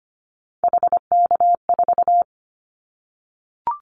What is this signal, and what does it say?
Morse code, heard as a keyed tone, about 25 wpm, 700 Hz. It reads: HK4
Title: Callsign Prefixes for Top 100 DXCC - 25wpm